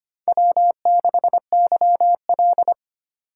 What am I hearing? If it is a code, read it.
Morse code, heard as a keyed tone, about 25 wpm, 700 Hz. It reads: W6YL
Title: USA Call Signs - 4 Characters - 2x Character Spacing - 25wpm